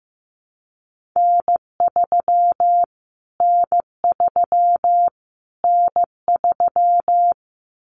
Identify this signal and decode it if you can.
Morse code, heard as a keyed tone, about 15 wpm, 700 Hz. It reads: N3 N3 N3
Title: Callsign Prefixes for Top 25 DXCC - Repeated 3x - 15wpm